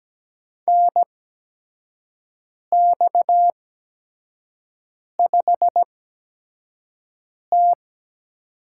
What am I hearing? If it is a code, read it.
Morse code, heard as a keyed tone, about 17 wpm, spaced as for 5 wpm, 700 Hz. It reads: NX5T
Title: USA Call Signs - 4 Characters - 8x Character Spacing - 17wpm